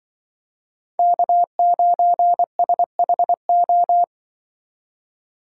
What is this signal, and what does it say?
Morse code, heard as a keyed tone, about 24 wpm, 700 Hz. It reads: K9SHO
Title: Speed Racing - USA 1x3 Call Signs 20wpm